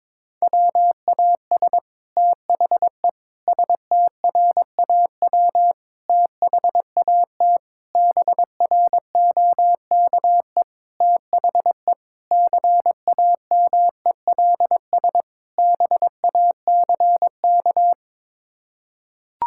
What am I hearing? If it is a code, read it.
Morse code, heard as a keyed tone, about 22 wpm, 700 Hz. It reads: WAS THE STRAW THAT BROKE THE CAMELS BACK
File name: practice / 22wpm / Sets of 8 Words - 22wpm